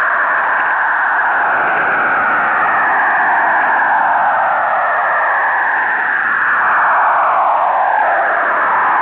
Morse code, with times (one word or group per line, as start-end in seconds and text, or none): none